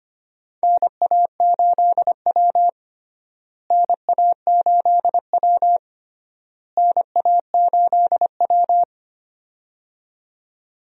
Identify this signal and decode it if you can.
Morse code, heard as a keyed tone, about 25 wpm, 700 Hz. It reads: NA8W NA8W NA8W
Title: USA Call Signs - 4 Characters - Repeated 3x - 25wpm